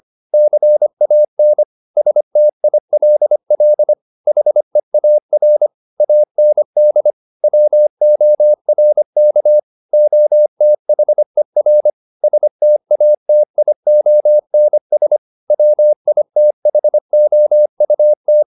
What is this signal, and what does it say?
Morse code, heard as a keyed tone, about 25 wpm, 600 Hz. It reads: CAN STILL HEAR AND WORK OTHER STATIONS WITHOUT